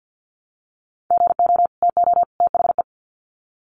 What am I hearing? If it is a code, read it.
Morse code, heard as a keyed tone, about 50 wpm, 700 Hz. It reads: GO TO THE
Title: Sets of 3 Words 50wpm